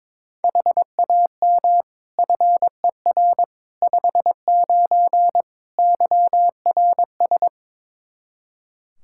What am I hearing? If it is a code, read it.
Morse code, heard as a keyed tone, about 22 wpm, 700 Hz. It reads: HAM FER 59 YRS
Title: Practice QSO - Exchange 3 - More Info - 22wpm